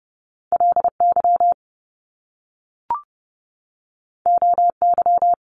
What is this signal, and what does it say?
Morse code, heard as a keyed tone, about 30 wpm, 700 Hz. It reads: LY OY
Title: Top Two Letter Combinations in Words 30wpm